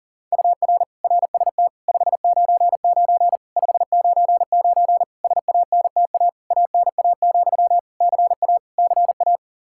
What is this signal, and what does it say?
Morse code, heard as a keyed tone, about 40 wpm, 700 Hz. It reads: UR RST 599 599 SANTA ANA, CA CA
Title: Practice QSO - Exchange 1 - Beginning - 40wpm